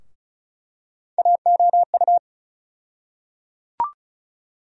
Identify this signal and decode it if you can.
Morse code, heard as a keyed tone, about 34 wpm, 700 Hz. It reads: AOU
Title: USA Call Sign Postfixes - 3 Letters 35wpm